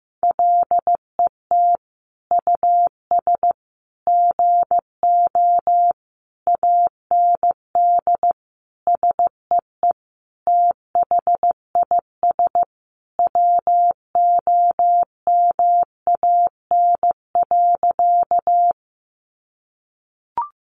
Morse code, heard as a keyed tone, about 15 wpm, 700 Hz. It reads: LET US GO AND SEE THIS WOMAN.
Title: Sentences from Top 1100 Words 15wpm